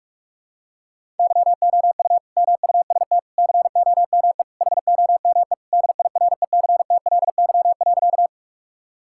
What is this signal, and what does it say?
Morse code, heard as a keyed tone, about 46 wpm, 700 Hz. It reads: YOU MUST COME HOME DIRECTLY.
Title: Sentences from Top 1000 Words - Review Part 5 - 45wpm